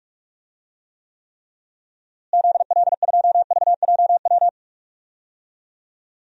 Morse code, heard as a keyed tone, about 46 wpm, 700 Hz. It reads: ZL2VJW
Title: DXCC-100 Call Signs - 6 Characters - Repeated 3x - 45wpm